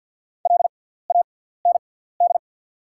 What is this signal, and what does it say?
Morse code, heard as a keyed tone, about 48 wpm, spaced as for 17 wpm, 690 Hz. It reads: LAND
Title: 6x Character Spacing - Top 500 Words - 50wpm